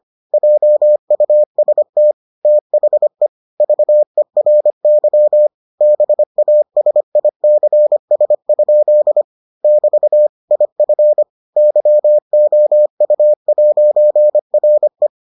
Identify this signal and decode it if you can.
Morse code, heard as a keyed tone, about 25 wpm, 600 Hz. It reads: JUST THE VERY BASICS? <BT> IF YOU'RE